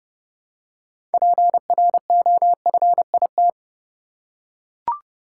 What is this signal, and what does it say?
Morse code, heard as a keyed tone, about 30 wpm, 700 Hz. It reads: PROFIT
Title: Six Letter Words - 30wpm